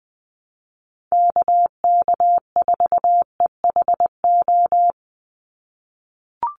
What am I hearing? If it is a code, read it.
Morse code, heard as a keyed tone, about 20 wpm, 700 Hz. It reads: KK4EHO
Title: USA Call Signs 20wpm